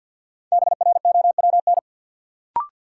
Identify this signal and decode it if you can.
Morse code, heard as a keyed tone, about 52 wpm, 700 Hz. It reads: BROWN
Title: Sentences from Top 1200 Words - New Word Drill Alt - 50wpm